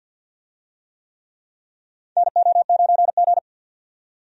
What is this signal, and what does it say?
Morse code, heard as a keyed tone, about 50 wpm, 700 Hz. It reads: NO9G
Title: USA Call Signs - 4 Characters - 50wpm